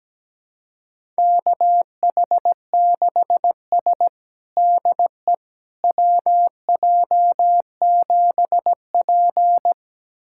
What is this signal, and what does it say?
Morse code, heard as a keyed tone, about 17 wpm, 700 Hz. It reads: KH6S DE WJ7P
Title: USA Callsign DE Callsign - 4 Characters - 17wpm